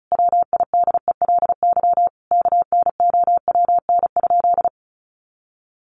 Morse code, heard as a keyed tone, about 34 wpm, 700 Hz. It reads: WIDELY KNOWN?
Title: Sentences from Top 1200 Words 35wpm